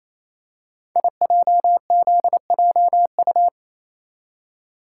Morse code, heard as a keyed tone, about 27 wpm, 700 Hz. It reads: IJZJU
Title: Code Groups - 5 Letter:Numbers - 28wpm